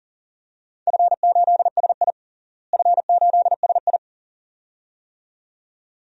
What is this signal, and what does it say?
Morse code, heard as a keyed tone, about 40 wpm, 700 Hz. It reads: F8SI F8SI
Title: DXCC-100 Call Signs - 4 Characters - Repeated 2x - 40wpm